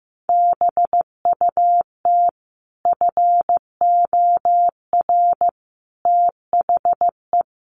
Morse code, heard as a keyed tone, about 15 wpm, 700 Hz. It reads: BUT FOR THE
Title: Sets of 5 Words Encore 15wpm